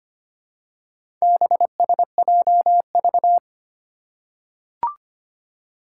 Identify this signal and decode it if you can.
Morse code, heard as a keyed tone, about 25 wpm, 700 Hz. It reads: BSJV
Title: Four Challenging Letter-Number Combinations 25wpm